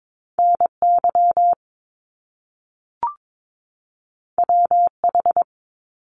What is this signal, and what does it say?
Morse code, heard as a keyed tone, about 22 wpm, 700 Hz. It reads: N Y W H
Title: Top Two Letter Combinations in Words 22wpm